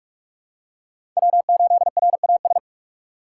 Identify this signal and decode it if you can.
Morse code, heard as a keyed tone, about 41 wpm, 700 Hz. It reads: W8RAS
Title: USA Call Signs - 5 Characters - 4x Character Spacing - 45wpm